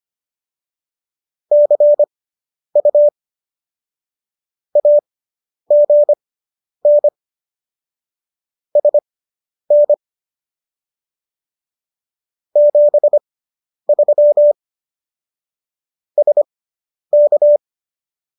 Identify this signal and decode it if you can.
Morse code, heard as a keyed tone, about 25 wpm, spaced as for 10 wpm, 590 Hz. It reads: CU AGN SN 73 SK